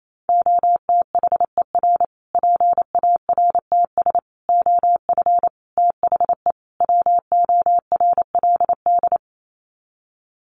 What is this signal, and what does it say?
Morse code, heard as a keyed tone, about 28 wpm, 700 Hz. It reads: OTHER PARTS OF THE WORLD
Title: Sets of 5 Words Encore 28wpm